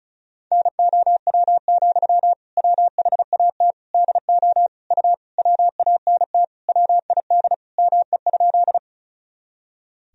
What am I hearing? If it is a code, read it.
Morse code, heard as a keyed tone, about 34 wpm, 700 Hz. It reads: NOW, WHAT DO U WANT WID ME?
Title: Sentences from Top 200 Words and Abbreviations 35wpm